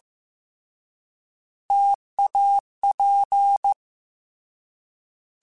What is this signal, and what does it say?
Morse code, heard as a keyed tone, about 15 wpm, 800 Hz. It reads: TAP
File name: Difficult Combos / Difficult Combos-mp3s / T AOEI @13